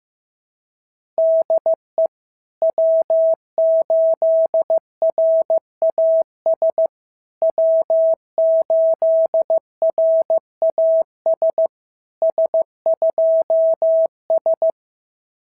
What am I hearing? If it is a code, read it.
Morse code, heard as a keyed tone, about 14 wpm, 650 Hz. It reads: DE W8RAS W8RAS S2S
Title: SOTA QSO Practice 15wpm